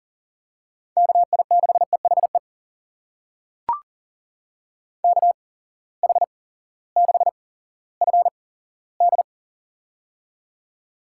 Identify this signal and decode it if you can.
Morse code, heard as a keyed tone, about 37 wpm, spaced as for 18 wpm, 700 Hz. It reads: KI6EHE KH6FD
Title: USA Call Signs - 8x Character Spacing - 40wpm